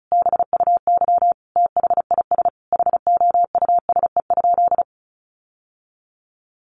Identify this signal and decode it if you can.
Morse code, heard as a keyed tone, about 34 wpm, 700 Hz. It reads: BUY THIS HOUSE?
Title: Sentences from Top 1400 Words 35wpm